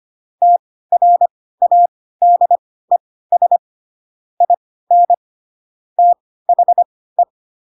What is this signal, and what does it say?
Morse code, heard as a keyed tone, about 25 wpm, spaced as for 16 wpm, 700 Hz. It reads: TRADES IN THE
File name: Pinocchio / Pinocchio ch4 10 25 ik1hge